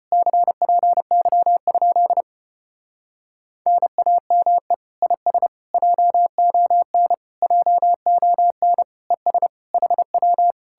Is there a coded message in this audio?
Yes: Morse code from 0.1 to 10.5 s, heard as a keyed tone, about 33 wpm, 700 Hz. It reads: CPY? NAME IS JON JON ES HW